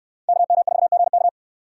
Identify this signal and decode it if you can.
Morse code, heard as a keyed tone, about 60 wpm, 700 Hz. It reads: KM4GK